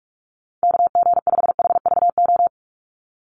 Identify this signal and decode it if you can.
Morse code, heard as a keyed tone, about 44 wpm, 700 Hz. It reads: KG5HVO
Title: Call Signs - 45wpm